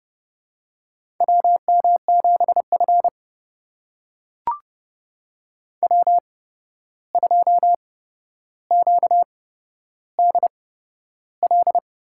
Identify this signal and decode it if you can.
Morse code, heard as a keyed tone, about 30 wpm, spaced as for 12 wpm, 700 Hz. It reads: WM7F W2QDL
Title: USA Call Signs - 8x Character Spacing - 30wpm